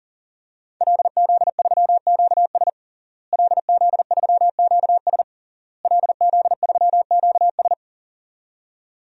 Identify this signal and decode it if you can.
Morse code, heard as a keyed tone, about 40 wpm, 700 Hz. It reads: LZ3QS LZ3QS LZ3QS
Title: DXCC-100 Call Signs - 5 Characters - Repeated 3x - 40wpm